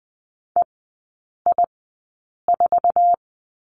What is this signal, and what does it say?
Morse code, heard as a keyed tone, about 20 wpm, spaced as for 8 wpm, 700 Hz. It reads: EI4
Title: 053 Lesson 10e - Letter r - Three Characters 20wpm